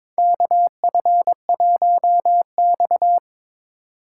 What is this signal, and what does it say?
Morse code, heard as a keyed tone, about 21 wpm, 700 Hz. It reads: KF1X
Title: USA Call Signs - 2x Character Spacing - 22wpm